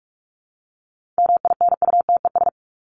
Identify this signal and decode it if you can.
Morse code, heard as a keyed tone, about 44 wpm, 700 Hz. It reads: MINUTES
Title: ES:S Suffix - Top 100 Words with Base Words 45wpm